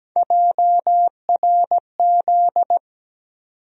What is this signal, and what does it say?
Morse code, heard as a keyed tone, about 17 wpm, 700 Hz. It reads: JRZ
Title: USA Call Sign Postfixes - 3 Letters 17wpm